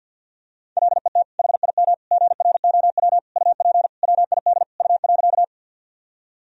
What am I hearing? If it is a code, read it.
Morse code, heard as a keyed tone, about 52 wpm, 700 Hz. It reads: LET HIM GROW UP WID U.
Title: Sentences from Top 300 Words and Abbreviations 50wpm